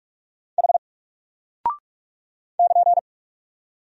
Morse code, heard as a keyed tone, about 44 wpm, 700 Hz. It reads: H <KN>
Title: Single Character (Letter:Number:Punctuation:Prosign) Alt 45wpm